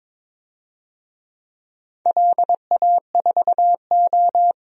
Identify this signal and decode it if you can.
Morse code, heard as a keyed tone, about 22 wpm, 700 Hz. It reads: LA4O
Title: DXCC-100 Call Signs - 4 Characters - 22wpm